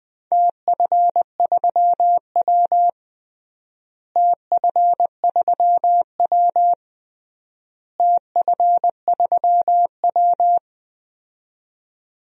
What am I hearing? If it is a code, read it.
Morse code, heard as a keyed tone, about 20 wpm, 700 Hz. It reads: TF3W TF3W TF3W
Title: DXCC-100 Call Signs - 4 Characters - Repeated 3x - 20wpm